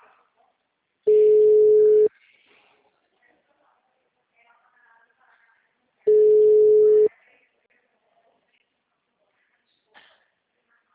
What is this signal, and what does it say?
Ringback tone (Europe, 425 Hz), with speech.
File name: out- -2014- - - .14